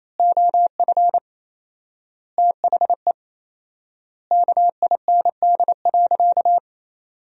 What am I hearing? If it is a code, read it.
Morse code, heard as a keyed tone, about 28 wpm, 700 Hz. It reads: OF THE KIND.
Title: 4x Word-Spacing - Sentences from Top 500 Words - 28wpm